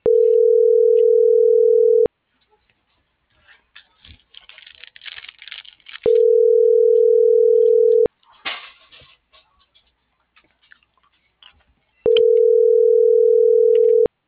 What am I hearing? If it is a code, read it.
Ringback tone (North America), 440 + 480 Hz.